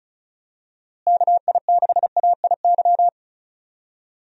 Morse code, heard as a keyed tone, about 34 wpm, 700 Hz. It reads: KI6AIY
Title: USA Call Signs - 6 Characters - Repeated 3x - 35wpm